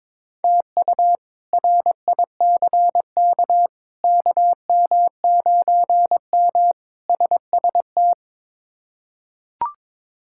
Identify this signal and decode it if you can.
Morse code, heard as a keyed tone, about 22 wpm, 700 Hz. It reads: TU RICK KM9M SST
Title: K1USN SST QSO Practice - 22wpm